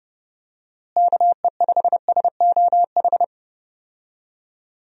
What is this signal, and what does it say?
Morse code, heard as a keyed tone, about 30 wpm, 700 Hz. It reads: KE5SOH
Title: USA Call Signs - 6 Characters - Repeated 3x - 30wpm